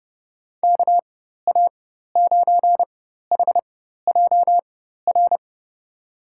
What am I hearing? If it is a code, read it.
Morse code, heard as a keyed tone, about 30 wpm, spaced as for 14 wpm, 700 Hz. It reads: KA9HJR
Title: USA Call Signs - 6 Characters - 4x Character Spacing - 30wpm